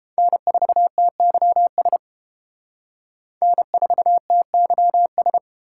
Morse code, heard as a keyed tone, about 31 wpm, 700 Hz. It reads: N4TYS N4TYS